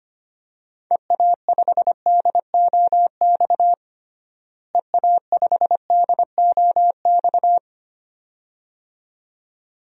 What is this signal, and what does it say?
Morse code, heard as a keyed tone, about 25 wpm, 700 Hz. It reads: EA5DOX EA5DOX
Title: DXCC-100 Call Signs - 6 Characters - Repeated 2x - 25wpm